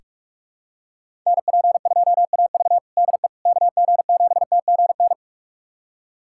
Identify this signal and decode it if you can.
Morse code, heard as a keyed tone, about 41 wpm, 700 Hz. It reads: NP2AV DE KG7TGN
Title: USA Callsign DE Callsign 45wpm